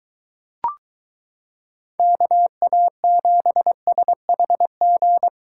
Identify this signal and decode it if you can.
Morse code, heard as a keyed tone, about 23 wpm, 700 Hz. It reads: KA7SHG